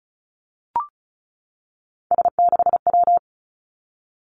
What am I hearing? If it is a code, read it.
Morse code, heard as a keyed tone, about 36 wpm, 700 Hz. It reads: S6W
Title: Three Challenging Letter-Number Combinations 35wpm